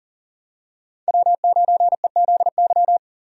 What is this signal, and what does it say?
Morse code, heard as a keyed tone, about 40 wpm, 700 Hz. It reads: W9EZY
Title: USA Call Signs - 5 Characters - 40wpm